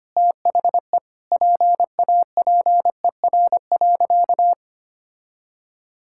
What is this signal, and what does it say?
Morse code, heard as a keyed tone, about 25 wpm, 700 Hz. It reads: THE PAPER.
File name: Easy Sentences with Hints - 25wpm